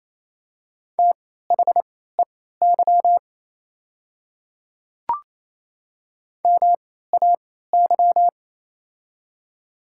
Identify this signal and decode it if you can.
Morse code, heard as a keyed tone, about 28 wpm, spaced as for 16 wpm, 700 Hz. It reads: THEY MAY
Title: 3x Character Spacing - Top 100 Words - 28wpm